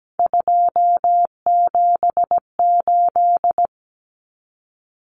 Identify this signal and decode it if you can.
Morse code, heard as a keyed tone, about 17 wpm, 700 Hz. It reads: 278
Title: Common Cut Numbers - 3-Digit 17wpm